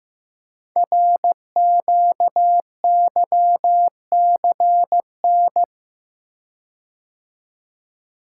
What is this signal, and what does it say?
Morse code, heard as a keyed tone, about 15 wpm, 700 Hz. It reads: RQYCN
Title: Code Groups - 5 Letters - 15wpm